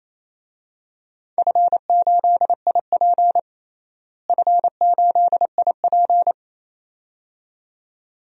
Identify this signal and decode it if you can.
Morse code, heard as a keyed tone, about 28 wpm, 700 Hz. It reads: F8IP F8IP
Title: DXCC-100 Call Signs - 4 Characters - Repeated 2x - 28wpm